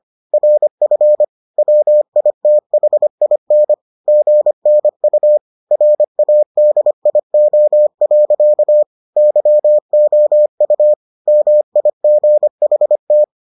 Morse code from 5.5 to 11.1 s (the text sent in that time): RADIO. YOU